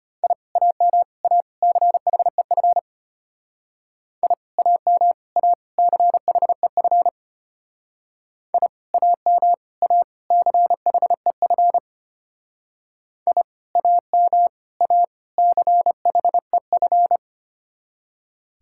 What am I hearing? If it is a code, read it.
Morse code, heard as a keyed tone, about 32 wpm, 700 Hz. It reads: I AM A CHEF I AM A CHEF I AM A CHEF I AM A CHEF